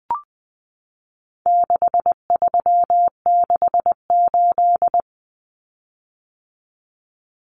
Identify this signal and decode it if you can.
Morse code, heard as a keyed tone, about 20 wpm, 700 Hz. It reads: 6368
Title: Common Cut Numbers - 4-Digit 20wpm